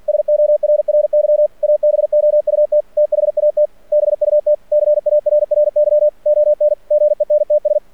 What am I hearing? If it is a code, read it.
Morse code, heard as a keyed tone, about 49 wpm, 600 Hz. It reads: SORRY ABOUT THAT BUT CARRY ON GENTS
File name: archived / CRSnet